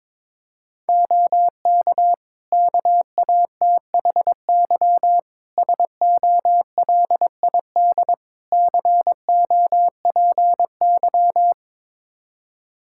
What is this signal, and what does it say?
Morse code, heard as a keyed tone, about 22 wpm, 700 Hz. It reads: OK KATHY SOLID COPY
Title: Practice QSO - Exchange 3 - More Info - 22wpm